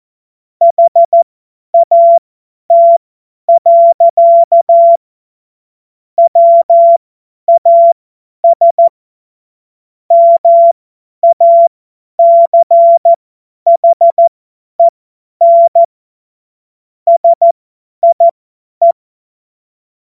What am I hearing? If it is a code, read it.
Morse code, heard as a keyed tone, about 14 wpm, spaced as for 10 wpm, 680 Hz. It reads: HAT. WAS MACHEN SIE
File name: Morsekurs / Witze / Witz 02 Januar 2024 50 F70